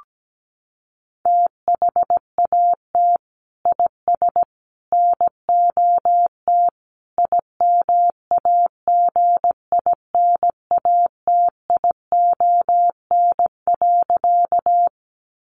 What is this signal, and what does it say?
Morse code, heard as a keyed tone, about 17 wpm, 700 Hz. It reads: THAT IS NOT IMAGINATION.
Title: Sentences from Top 1300 Words 17wpm